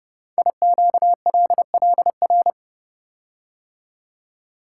Morse code, heard as a keyed tone, about 29 wpm, 700 Hz. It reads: IQLLR
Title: Code Groups - 5 Letters - 30wpm